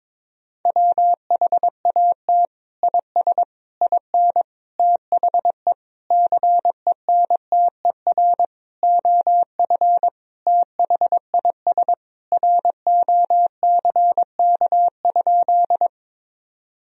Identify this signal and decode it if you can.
Morse code, heard as a keyed tone, about 22 wpm, 700 Hz. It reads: WHAT IS IN THE CENTER OF THIS ROCK?
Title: Sentences from Top 1000 Words - Review Part 2 - 22wpm